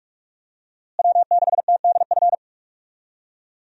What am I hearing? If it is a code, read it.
Morse code, heard as a keyed tone, about 44 wpm, 700 Hz. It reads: W6TDF
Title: USA Call Signs - 5 Characters - Repeated 3x - 45wpm